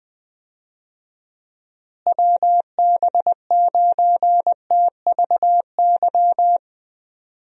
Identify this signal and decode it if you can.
Morse code, heard as a keyed tone, about 20 wpm, 700 Hz. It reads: WB9TVY
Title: USA Call Signs - 6 Characters - Repeated 2x - 20wpm